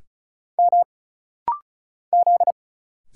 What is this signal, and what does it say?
Morse code, heard as a keyed tone, about 35 wpm, 700 Hz. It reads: M Z
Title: Single Letter-Number - Rapid-Fire 35wpm